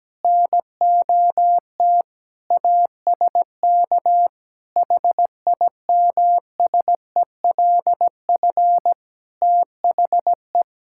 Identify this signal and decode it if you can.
Morse code, heard as a keyed tone, about 17 wpm, 700 Hz. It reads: NOT ASK HIMSELF THE
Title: Sentences from Top 1600 Words 17wpm